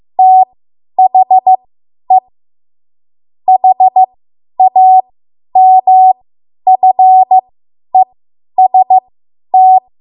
Morse code, heard as a keyed tone, about 15 wpm, spaced as for 10 wpm, 750 Hz. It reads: THE HAMFEST
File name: Morse / 10WPM